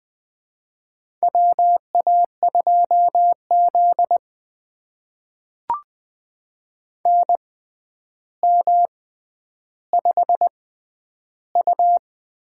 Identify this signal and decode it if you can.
Morse code, heard as a keyed tone, about 20 wpm, spaced as for 10 wpm, 700 Hz. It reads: WA2Z NM5U
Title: USA Call Signs - 4 Characters - 6x Character Spacing - 20wpm